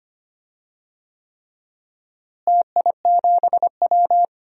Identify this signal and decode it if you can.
Morse code, heard as a keyed tone, about 25 wpm, 700 Hz. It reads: TI7W
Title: DXCC-100 Call Signs - 4 Characters - Repeated 3x - 25wpm